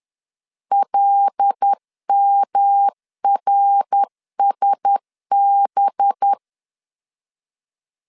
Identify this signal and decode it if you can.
Morse code, heard as a keyed tone, about 10 wpm, 800 Hz. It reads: LMRSB